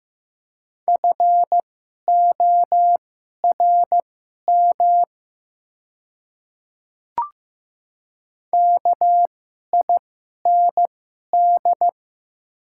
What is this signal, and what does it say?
Morse code, heard as a keyed tone, about 15 wpm, spaced as for 11 wpm, 700 Hz. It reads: FORM KIND
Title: 2x Character Spacing - Top 200 Words - 15wpm